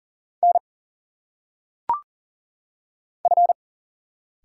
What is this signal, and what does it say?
Morse code, heard as a keyed tone, about 40 wpm, 700 Hz. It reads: N F
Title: Single Letters - Phonetic Pronunciation 40wpm